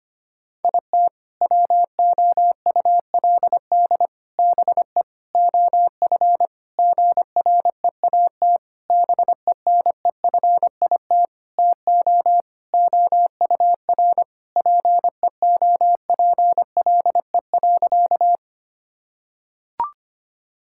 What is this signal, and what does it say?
Morse code, heard as a keyed tone, about 25 wpm, 700 Hz. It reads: IT WOULD BE OF GREAT BENEFIT TO OUR PEOPLE.
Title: Sentences from Top 1000 Words - Review Part 6 - 25wpm